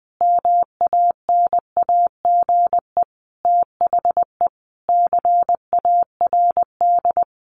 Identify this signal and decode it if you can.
Morse code, heard as a keyed tone, about 20 wpm, 700 Hz. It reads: MANAGE THE CARD
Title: Sentences from Top 1500 Words 20wpm